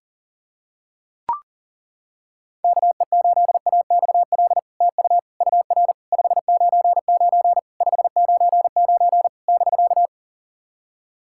Morse code, heard as a keyed tone, about 39 wpm, 700 Hz. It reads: KE8AXL TU UR 599 599 <BK>
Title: SOTA QSO Practice 40wpm